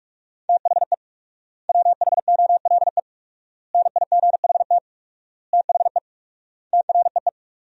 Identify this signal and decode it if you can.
Morse code, heard as a keyed tone, about 45 wpm, 690 Hz. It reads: THE WHOLE NIGHT THE TREE
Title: 4x Word-Spacing - Sentences from Top 500 Words - 45wpm